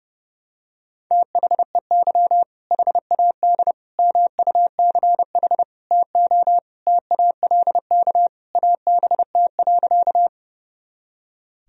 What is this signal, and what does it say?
Morse code, heard as a keyed tone, about 32 wpm, 700 Hz. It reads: THEY HAD MUCH TO TALK ABT.
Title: Sentences from Top 400 Words and Abbreviations 30wpm